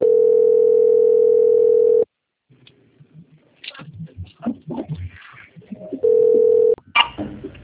Ringback tone (North America), 440 + 480 Hz.
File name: out- - - - - .31509